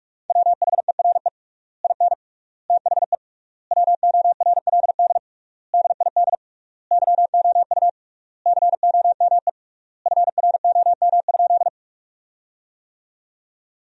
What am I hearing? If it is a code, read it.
Morse code, heard as a keyed tone, about 46 wpm, 700 Hz. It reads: WHERE IN THE WORLD DID YOU COME FROM?
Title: 3x Word-Spacing - Sentences from Top 200 Words - 45wpm